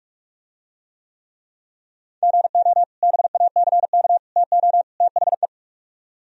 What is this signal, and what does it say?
Morse code, heard as a keyed tone, about 42 wpm, 700 Hz. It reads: GO BACK TO THE